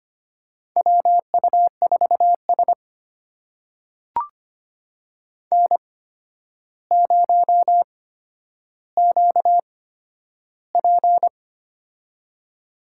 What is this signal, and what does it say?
Morse code, heard as a keyed tone, about 25 wpm, spaced as for 11 wpm, 700 Hz. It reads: WU4S N0QP
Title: USA Call Signs - 4 Characters - 8x Character Spacing - 25wpm